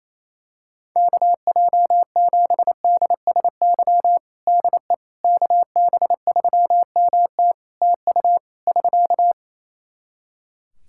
Morse code, heard as a keyed tone, about 28 wpm, 700 Hz. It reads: KJ7DSY DE KB3MT TU <SK>
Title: K1USN SST QSO Practice 28wpm